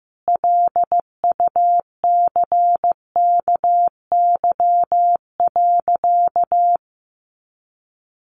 Morse code, heard as a keyed tone, about 15 wpm, 700 Hz. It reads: LUCKY.
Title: A Sound Of Thunder 15wpm